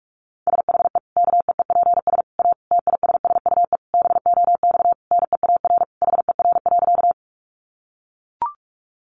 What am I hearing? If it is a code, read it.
Morse code, heard as a keyed tone, about 44 wpm, 700 Hz. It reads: SHE KEEPS A TISSUE BOX NEAR HER.
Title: Easy Phrases 45wpm